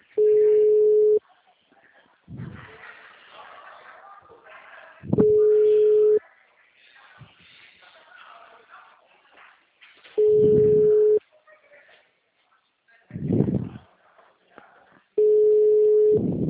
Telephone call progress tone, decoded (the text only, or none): ringback tone